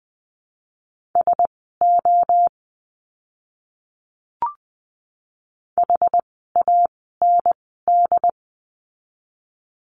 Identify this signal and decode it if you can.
Morse code, heard as a keyed tone, about 20 wpm, spaced as for 14 wpm, 700 Hz. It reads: SO HAND
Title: 115 Lesson 19f - Letter w - Words 20wpm